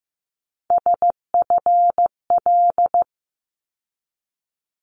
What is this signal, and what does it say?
Morse code, heard as a keyed tone, about 15 wpm, 700 Hz. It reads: SFL
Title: ARRL Sections 15wpm